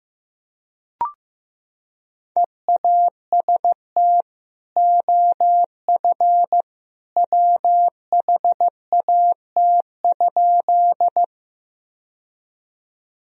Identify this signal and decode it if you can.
Morse code, heard as a keyed tone, about 16 wpm, 700 Hz. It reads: EAST OF WHAT?
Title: Sentences from Top 1000 Words - Review Part 3 - 15wpm